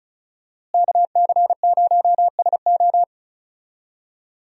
Morse code, heard as a keyed tone, about 36 wpm, 700 Hz. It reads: KC0SO
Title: USA Call Signs - 5 Characters - Repeated 2x - 35wpm